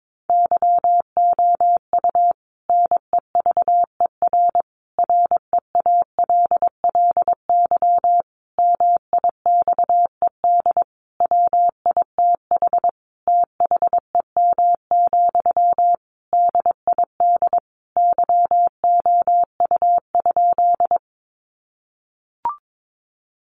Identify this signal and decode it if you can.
Morse code, heard as a keyed tone, about 22 wpm, 700 Hz. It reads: YOU NEVER REALLY MIXED WITH THEM, DID YOU?
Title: Sentences from Top 1500 Words 22wpm